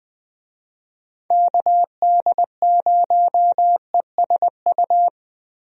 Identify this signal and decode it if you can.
Morse code, heard as a keyed tone, about 21 wpm, 700 Hz. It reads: KD0ESU
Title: USA Call Signs - 6 Characters - Repeated 3x - 20wpm